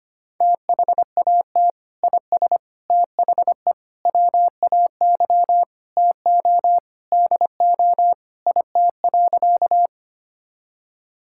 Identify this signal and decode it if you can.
Morse code, heard as a keyed tone, about 25 wpm, 700 Hz. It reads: THAT IS THE WAY TO DO IT.
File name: practice / 25wpm / Sentences with 3-Word Phrases - 25wpm